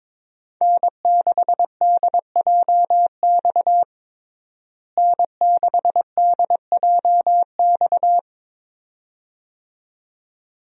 Morse code, heard as a keyed tone, about 22 wpm, 700 Hz. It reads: N6DJX N6DJX
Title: USA Call Signs - 5 Characters - Repeated 2x - 22wpm